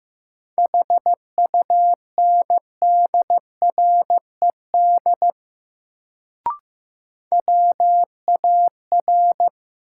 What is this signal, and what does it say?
Morse code, heard as a keyed tone, about 16 wpm, 700 Hz. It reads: HUNDRED WAR
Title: Sentences from Top 400 Words - New Word Drill - 15wpm